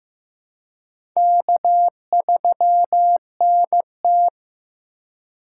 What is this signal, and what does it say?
Morse code, heard as a keyed tone, about 15 wpm, 700 Hz. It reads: K3NT
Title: USA Call Signs - 3x Character Spacing - 15wpm